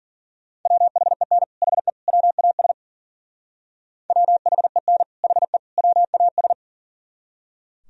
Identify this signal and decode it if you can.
Morse code, heard as a keyed tone, about 52 wpm, 700 Hz. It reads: WHEN HE WAS WHEN HE WAS